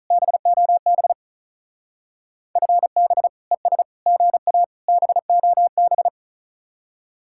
Morse code, heard as a keyed tone, about 45 wpm, 700 Hz. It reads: BOB FB ES GA BOB